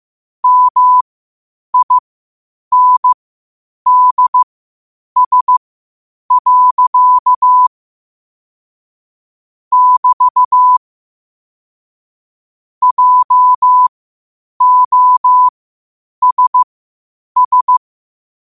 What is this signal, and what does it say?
Morse code, heard as a keyed tone, about 15 wpm, spaced as for 8 wpm, 1000 Hz. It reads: MINDS. <BT> JOSS
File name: qotdfiles / 8wpm